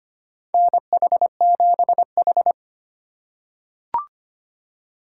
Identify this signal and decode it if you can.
Morse code, heard as a keyed tone, about 25 wpm, 700 Hz. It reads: NH7H
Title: USA Call Signs - 4 Characters - 2x Character Spacing - 25wpm